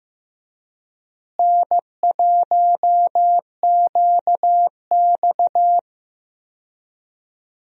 Morse code, heard as a keyed tone, about 15 wpm, 700 Hz. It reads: N1QX
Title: USA Call Signs - 4 Characters - Repeated 3x - 15wpm